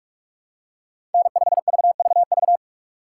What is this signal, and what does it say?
Morse code, heard as a keyed tone, about 45 wpm, 700 Hz. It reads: N5VVV
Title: USA Call Signs - 5 Characters - 6x Character Spacing - 45wpm